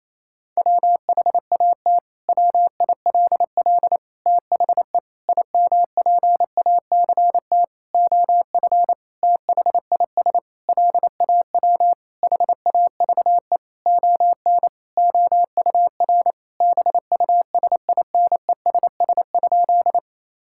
Morse code, heard as a keyed tone, about 27 wpm, 700 Hz. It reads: WHAT WILL THE IMPACT OF THIS LAW HAVE ON OUR BUSINESS?
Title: Sentences from Top 1000 Words - Review Part 6 - 28wpm